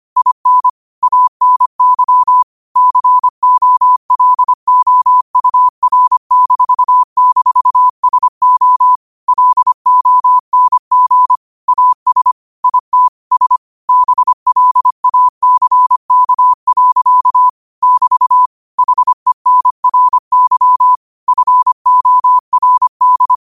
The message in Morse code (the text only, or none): IN ANY COLOUR--SO LONG AS IT S BLACK. <BT> HENRY FORD